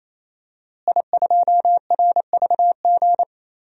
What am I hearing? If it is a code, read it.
Morse code, heard as a keyed tone, about 28 wpm, 700 Hz. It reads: I2RVG
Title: DXCC-100 Call Signs - 5 Characters - Repeated 3x - 28wpm